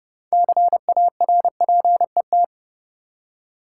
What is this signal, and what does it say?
Morse code, heard as a keyed tone, about 32 wpm, 700 Hz. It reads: CARPET
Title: Six Letter Words - 30wpm